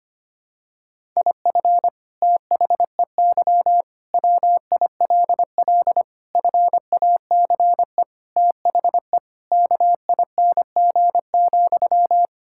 Morse code, heard as a keyed tone, about 25 wpm, 700 Hz. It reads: IF THEY WILL FACE THE KING,
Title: Sentences from Top 1000 Words - Review Part 6 - 25wpm